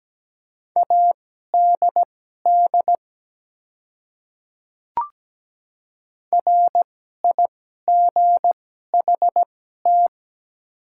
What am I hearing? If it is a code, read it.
Morse code, heard as a keyed tone, about 17 wpm, spaced as for 12 wpm, 700 Hz. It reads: ADD RIGHT
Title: 2x Character Spacing - Top 200 Words - 17wpm